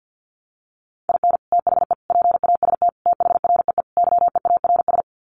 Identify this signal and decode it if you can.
Morse code, heard as a keyed tone, about 50 wpm, 700 Hz. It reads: IN THE PAST THREE YEARS
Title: Sets of 5 Words Encore2 50wpm